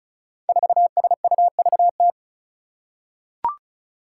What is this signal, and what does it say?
Morse code, heard as a keyed tone, about 36 wpm, 700 Hz. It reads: 4SUVT
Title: Five Challenging Letter-Number Combinations 35wpm